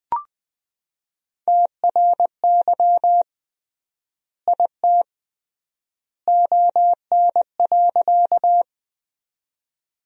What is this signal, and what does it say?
Morse code, heard as a keyed tone, about 20 wpm, 700 Hz. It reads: TRY IT ON.
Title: 3x Word-Spacing - Sentences from Top 200 Words - 20wpm